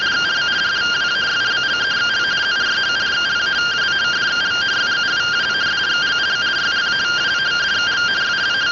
Two-tone teleprinter signal, about 170 Hz shift.
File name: findings / CT 205 ax mips, iDose 3 0003.gen-08 rtty rx